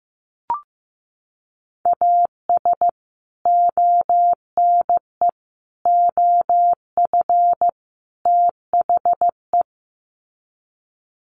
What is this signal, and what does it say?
Morse code, heard as a keyed tone, about 15 wpm, 700 Hz. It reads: AS ONE OF THE
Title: Sets of 4 Words 15wpm